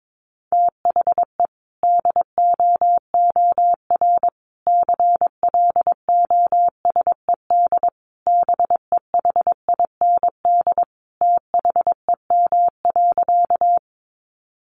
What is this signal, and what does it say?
Morse code, heard as a keyed tone, about 22 wpm, 700 Hz. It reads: THE DOOR CLOSED BEHIND THEM.
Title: Sentences from Top 1400 Words 22wpm